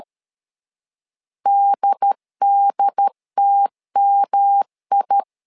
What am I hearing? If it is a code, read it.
Morse code, heard as a keyed tone, about 13 wpm, 800 Hz. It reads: DDTMI
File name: LEZIONI / 3 MWRO / 3 M 12wpm 1541